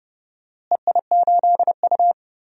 Morse code, heard as a keyed tone, about 30 wpm, 700 Hz. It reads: EI8U
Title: DXCC-100 Call Signs - 4 Characters - Repeated 2x - 30wpm